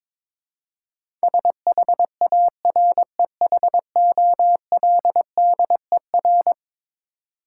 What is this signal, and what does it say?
Morse code, heard as a keyed tone, about 21 wpm, 700 Hz. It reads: SHAREHOLDER
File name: practice / 22wpm / Eleven Letter Words - 22wpm